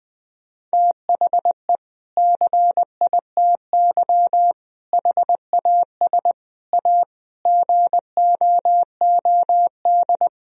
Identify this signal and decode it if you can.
Morse code, heard as a keyed tone, about 20 wpm, 700 Hz. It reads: THE CITY HAS A GOOD
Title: Sentences from Top 1000 Words - Review Part 4 - 20wpm